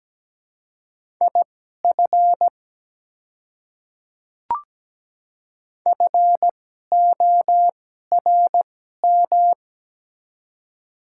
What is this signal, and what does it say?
Morse code, heard as a keyed tone, about 17 wpm, spaced as for 12 wpm, 700 Hz. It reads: IF FORM
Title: 2x Character Spacing - Top 200 Words - 17wpm